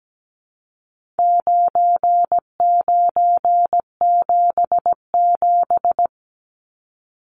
Morse code, heard as a keyed tone, about 17 wpm, 700 Hz. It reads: 9977
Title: Numbers - 4-Digit 17wpm